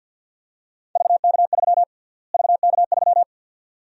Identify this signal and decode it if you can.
Morse code, heard as a keyed tone, about 49 wpm, 700 Hz. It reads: VK3 VK3
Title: USA Call Sign Prefixes - Repeated 2x - 50wpm